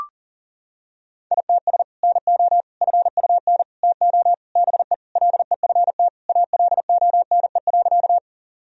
Morse code, heard as a keyed tone, about 40 wpm, 700 Hz. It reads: ITS NO FUN TO BE LEFT ALONE.
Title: Sentences from Top 1000 Words - Review Part 4 - 40wpm